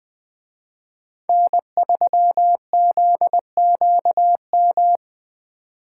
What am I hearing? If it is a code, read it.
Morse code, heard as a keyed tone, about 20 wpm, 700 Hz. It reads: N3ZQM
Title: Speed Racing - USA 1x3 Call Signs 17wpm